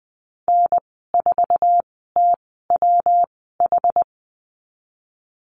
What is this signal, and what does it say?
Morse code, heard as a keyed tone, about 20 wpm, spaced as for 14 wpm, 700 Hz. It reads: N4TWH
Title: 122 Lesson 20g - Number 3 - Call Signs 20wpm